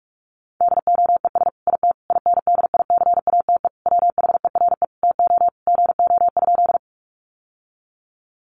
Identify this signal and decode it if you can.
Morse code, heard as a keyed tone, about 44 wpm, 700 Hz. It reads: DOES IT INDICATE WHERE TO GO?
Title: Sentences from Top 1300 Words 45wpm